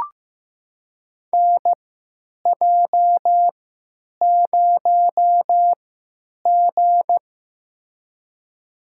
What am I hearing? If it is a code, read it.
Morse code, heard as a keyed tone, about 15 wpm, spaced as for 9 wpm, 700 Hz. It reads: NJ0G
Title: USA Call Signs - 4 Characters - 3x Character Spacing - 15wpm